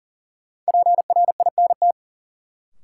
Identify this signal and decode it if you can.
Morse code, heard as a keyed tone, about 41 wpm, 700 Hz. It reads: PRINT
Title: Sentences from Top 1400 Words - New Word Drill - 40wpm